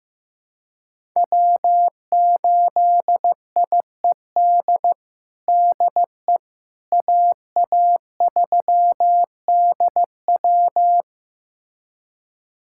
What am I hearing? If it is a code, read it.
Morse code, heard as a keyed tone, about 15 wpm, 700 Hz. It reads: W8IED DE AA3DW
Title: USA Callsign DE Callsign - 5 Characters - 15wpm